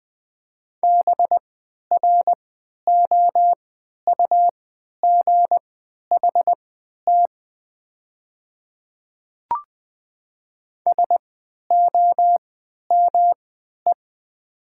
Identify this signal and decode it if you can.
Morse code, heard as a keyed tone, about 20 wpm, spaced as for 11 wpm, 700 Hz. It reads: BROUGHT SOME
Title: 3x Character Spacing - Top 500 Words - 20wpm